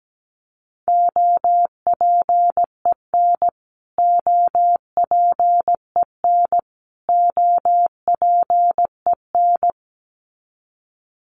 Morse code, heard as a keyed tone, about 17 wpm, 700 Hz. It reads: OPEN OPEN OPEN
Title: Top 500 Words Repeated 17wpm